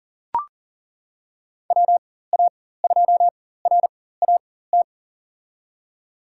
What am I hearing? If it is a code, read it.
Morse code, heard as a keyed tone, about 40 wpm, spaced as for 19 wpm, 700 Hz. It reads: WA2RAT
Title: USA Call Signs - 4x Character Spacing - 40wpm